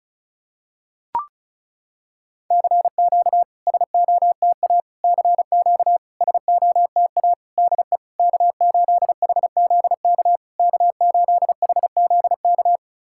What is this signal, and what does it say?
Morse code, heard as a keyed tone, about 36 wpm, 690 Hz. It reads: CQ SOTA CQ SOTA DE K8HZK K8HZK
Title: SOTA QSO Practice 35wpm